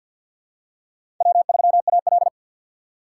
Morse code, heard as a keyed tone, about 50 wpm, 700 Hz. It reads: W3AL
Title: USA Call Signs - 4 Characters - 2x Character Spacing - 50wpm